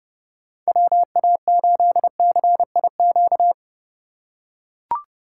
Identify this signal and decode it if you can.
Morse code, heard as a keyed tone, about 31 wpm, 700 Hz. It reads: WA8CIQ